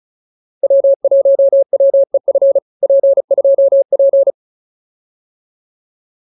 Morse code, heard as a keyed tone, about 34 wpm, 590 Hz. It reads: W1WEF P2P
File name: POTA306 35